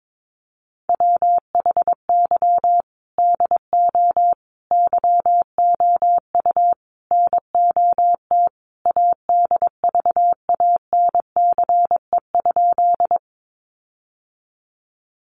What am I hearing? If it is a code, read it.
Morse code, heard as a keyed tone, about 22 wpm, 700 Hz. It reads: WHY DO YOU NOT ADVANCE?
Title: Sentences from Top 1400 Words 22wpm